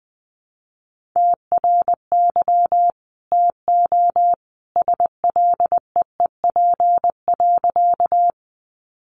Morse code, heard as a keyed tone, about 20 wpm, 700 Hz. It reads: TRY TO SLEEP.
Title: Sentences from Top 500 Words 20wpm